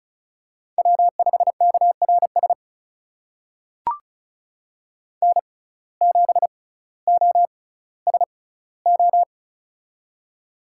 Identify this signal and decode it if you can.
Morse code, heard as a keyed tone, about 36 wpm, spaced as for 18 wpm, 700 Hz. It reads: W5KRS N7OSO
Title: USA Call Signs - 5 Characters - 6x Character Spacing - 35wpm